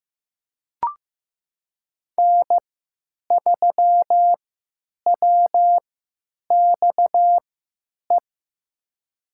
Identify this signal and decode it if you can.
Morse code, heard as a keyed tone, about 15 wpm, spaced as for 9 wpm, 700 Hz. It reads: N3WXE
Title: USA Call Signs - 5 Characters - 3x Character Spacing - 15wpm